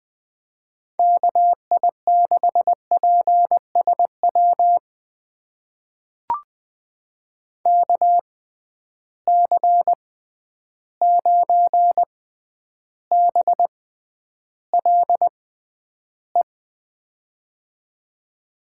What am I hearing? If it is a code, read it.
Morse code, heard as a keyed tone, about 20 wpm, spaced as for 10 wpm, 700 Hz. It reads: KI6PSW KC9BLE
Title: USA Call Signs - 6 Characters - 6x Character Spacing - 20wpm